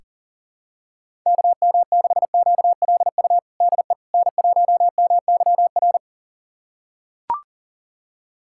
Morse code, heard as a keyed tone, about 40 wpm, 700 Hz. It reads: KM6QLU DE N1MYR
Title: USA Callsign DE Callsign 40wpm